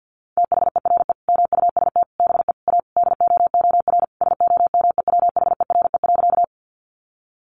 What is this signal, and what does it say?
Morse code, heard as a keyed tone, about 50 wpm, 700 Hz. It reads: THERE MUST BE A DOOR SOMEWHERE.
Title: Sentences from Top 1400 Words 50wpm